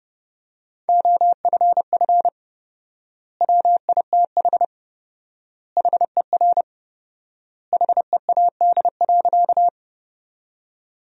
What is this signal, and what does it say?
Morse code, heard as a keyed tone, about 30 wpm, 700 Hz. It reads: OFF WITH HER HEAD.
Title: 4x Word-Spacing - Sentences from Top 300 Words - 30wpm